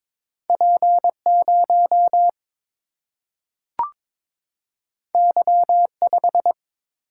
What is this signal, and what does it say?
Morse code, heard as a keyed tone, about 22 wpm, 700 Hz. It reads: P 0 Y 5
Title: Single-Letter Single-Number 22wpm